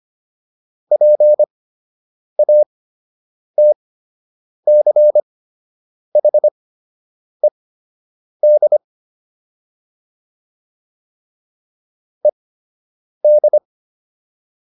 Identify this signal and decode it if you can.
Morse code, heard as a keyed tone, about 25 wpm, spaced as for 8 wpm, 600 Hz. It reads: PATCHED ED